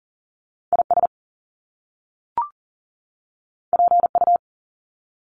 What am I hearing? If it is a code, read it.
Morse code, heard as a keyed tone, about 40 wpm, 700 Hz. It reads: IS PU
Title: Top Two Letter Combinations in Words 40wpm